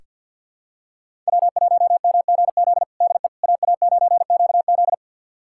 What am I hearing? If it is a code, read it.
Morse code, heard as a keyed tone, about 47 wpm, 700 Hz. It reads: W1MGZ DE AA9QZ
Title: USA Callsign DE Callsign - 5 Characters - 50wpm